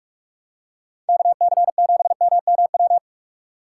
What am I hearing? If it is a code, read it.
Morse code, heard as a keyed tone, about 44 wpm, 700 Hz. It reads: KC7MMW
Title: USA Call Signs - 6 Characters - 6x Character Spacing - 45wpm